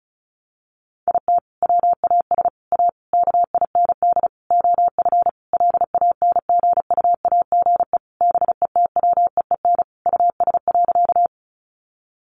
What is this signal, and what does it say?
Morse code, heard as a keyed tone, about 34 wpm, 700 Hz. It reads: IT WAS A KIND OF LANGUAGE BETWEEN US.
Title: Sentences from Top 500 Words 35wpm